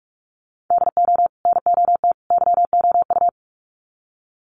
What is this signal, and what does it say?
Morse code, heard as a keyed tone, about 44 wpm, 700 Hz. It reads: DO NOT YOU
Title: Sets of 3 Words 45wpm